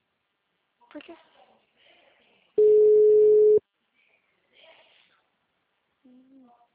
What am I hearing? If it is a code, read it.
Ringback tone (Europe, 425 Hz), with speech.